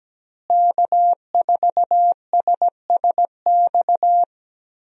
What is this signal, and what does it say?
Morse code, heard as a keyed tone, about 17 wpm, 700 Hz. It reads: K4SSX
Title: USA Call Signs - 5 Characters - Repeated 3x - 17wpm